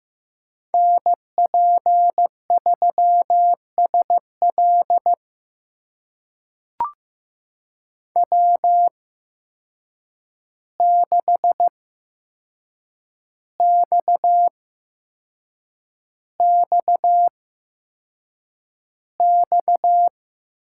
Morse code, heard as a keyed tone, about 15 wpm, spaced as for 6 wpm, 700 Hz. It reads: NP3SL W6XXX
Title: USA Call Signs - 5 Characters - 8x Character Spacing - 15wpm